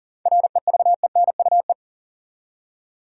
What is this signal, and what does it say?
Morse code, heard as a keyed tone, about 40 wpm, 700 Hz. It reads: REVENUE
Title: Seven Letter Words - 40wpm